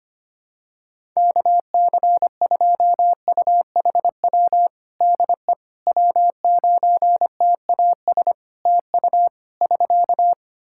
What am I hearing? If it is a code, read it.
Morse code, heard as a keyed tone, about 25 wpm, 700 Hz. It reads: KC2UHW DE W9TAS TU <SK>
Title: Practice QSO - Exchange 4 - Ending - 25wpm